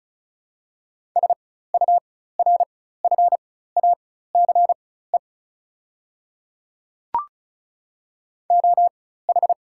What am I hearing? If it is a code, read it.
Morse code, heard as a keyed tone, about 36 wpm, spaced as for 16 wpm, 690 Hz. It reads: SURFACE OH
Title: 4x Character Spacing - Top 500 Words - 35wpm